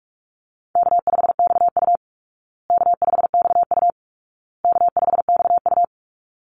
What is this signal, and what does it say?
Morse code, heard as a keyed tone, about 44 wpm, 700 Hz. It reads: K5XU K5XU K5XU
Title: Easy Call Signs Repeated 3x - 45wpm